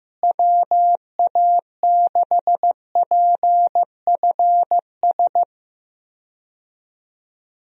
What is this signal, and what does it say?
Morse code, heard as a keyed tone, about 15 wpm, 700 Hz. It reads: WA6PFS
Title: USA Call Signs - 6 Characters - Repeated 2x - 15wpm